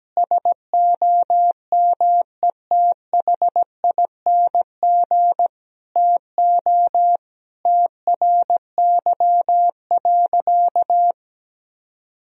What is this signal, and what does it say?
Morse code, heard as a keyed tone, about 17 wpm, 700 Hz. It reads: SOMETHING TO TRY.
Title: Sentences from Top 1000 Words - Review Part 4 - 17wpm